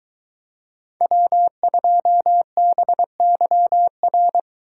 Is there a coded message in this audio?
Yes: Morse code from 1.0 to 4.4 s, heard as a keyed tone, about 21 wpm, 700 Hz. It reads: W2BYR